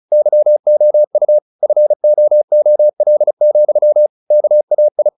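Morse code, heard as a keyed tone, about 34 wpm, 600 Hz. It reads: YOU FOOL, KAI
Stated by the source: Morse practice at 35 wpm from CW med Gnister, sending Ken Follett's novel Never in English.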